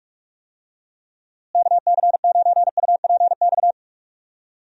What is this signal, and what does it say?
Morse code, heard as a keyed tone, about 46 wpm, 700 Hz. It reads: KC9UPX
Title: USA Call Signs - 6 Characters - Repeated 2x - 45wpm